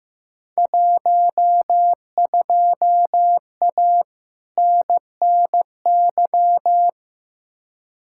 Morse code, heard as a keyed tone, about 15 wpm, 700 Hz. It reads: 12A NNY
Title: Field Day Exchanges Encore - 15wpm